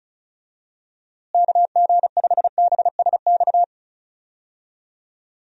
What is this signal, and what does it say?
Morse code, heard as a keyed tone, about 36 wpm, 700 Hz. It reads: KG5BSX
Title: USA Call Signs - 6 Characters - Repeated 3x - 35wpm